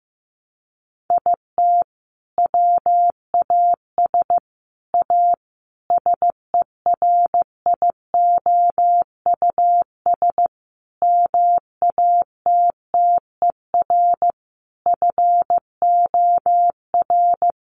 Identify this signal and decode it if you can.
Morse code, heard as a keyed tone, about 15 wpm, 700 Hz. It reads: IT WAS A SERIOUS MATTER FOR
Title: Sentences from Top 1100 Words 15wpm